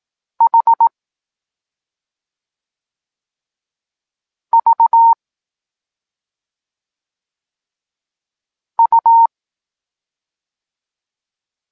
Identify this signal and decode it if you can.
Morse code, heard as a keyed tone, about 18 wpm, spaced as for 2 wpm, 920 Hz. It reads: HVU
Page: Tekst 8 – doskonalenie odbioru poznanych liter